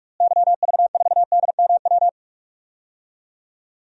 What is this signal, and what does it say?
Morse code, heard as a keyed tone, about 46 wpm, 700 Hz. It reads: YV4DMW
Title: DXCC-100 Call Signs - 6 Characters - Repeated 2x - 45wpm